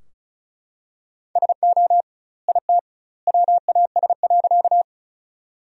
Morse code, heard as a keyed tone, about 36 wpm, 700 Hz. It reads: SO IT WAS.
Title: 2x Word-Spacing - Sentences from Top 100 Words - 35wpm